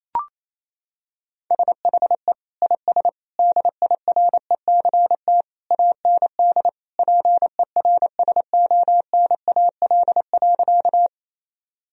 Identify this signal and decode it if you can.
Morse code, heard as a keyed tone, about 28 wpm, 700 Hz. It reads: SHE IS DIRECT AND PERSONAL.
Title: Sentences from Top 1000 Words - Review Part 4 - 28wpm